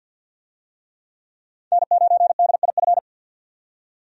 Morse code, heard as a keyed tone, about 51 wpm, 700 Hz. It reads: N9DIF
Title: USA Call Signs - 5 Characters - 50wpm